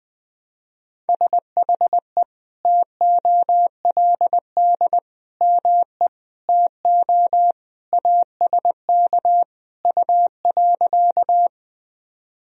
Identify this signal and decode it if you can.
Morse code, heard as a keyed tone, about 20 wpm, 700 Hz. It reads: SHE TOLD ME TO ASK U.
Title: Sentences from Top 400 Words and Abbreviations 20wpm